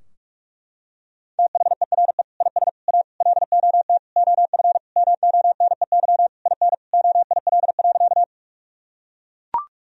Morse code, heard as a keyed tone, about 41 wpm, 700 Hz. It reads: THERE IS A LOT OF MONEY IN OIL.
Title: Sentences from Top 1000 Words - Review Part 4 - 45wpm